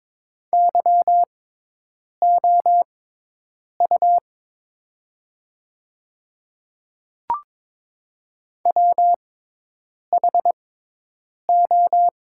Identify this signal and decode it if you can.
Morse code, heard as a keyed tone, about 22 wpm, spaced as for 8 wpm, 700 Hz. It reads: YOU WHO
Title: 6x Character Spacing - Top 100 Words - 22wpm